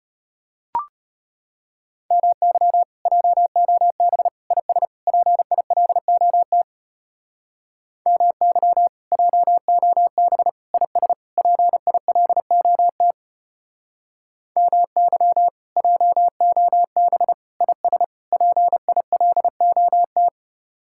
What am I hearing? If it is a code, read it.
Morse code, heard as a keyed tone, about 31 wpm, 700 Hz. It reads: MY JOB IS PILOT MY JOB IS PILOT MY JOB IS PILOT
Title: Speed-Racing Common QSO Phrases - 25wpm